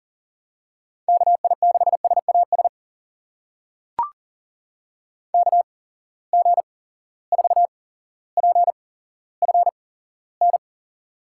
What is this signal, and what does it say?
Morse code, heard as a keyed tone, about 39 wpm, spaced as for 17 wpm, 700 Hz. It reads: KI6SAS KG4PFN
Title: USA Call Signs - 6 Characters - 8x Character Spacing - 40wpm